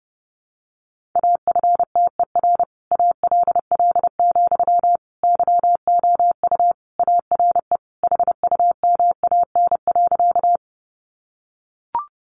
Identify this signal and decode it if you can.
Morse code, heard as a keyed tone, about 28 wpm, 700 Hz. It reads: AFTER ALL, YOU ARE HUMAN.
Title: Sentences from Top 600 Words 30wpm